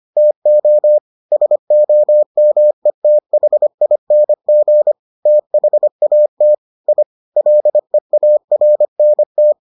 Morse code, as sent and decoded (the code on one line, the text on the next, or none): - --- / ... --- -- . - .... .. -. --. / - .... .- - / .. / .-.. .- .-. -. -
TO SOMETHING THAT I LEARNT